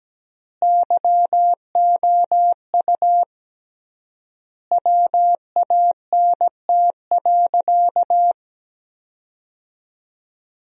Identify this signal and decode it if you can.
Morse code, heard as a keyed tone, about 17 wpm, 700 Hz. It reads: YOU WANT.
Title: 3x Word-Spacing - Sentences from Top 200 Words - 17wpm